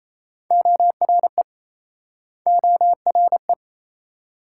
Morse code, heard as a keyed tone, about 39 wpm, 700 Hz. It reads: ORE ORE